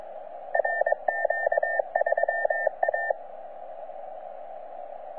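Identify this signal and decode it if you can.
Morse code, heard as a keyed tone, about 22 wpm, 1800 Hz, with radio noise. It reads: RQ3A
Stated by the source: C91RF DXpedition station heard on 10 m CW, from DX World.net.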